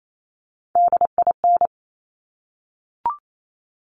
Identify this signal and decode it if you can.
Morse code, heard as a keyed tone, about 28 wpm, 700 Hz. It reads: DIN
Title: Top Three Letter Combinations in Words 28wpm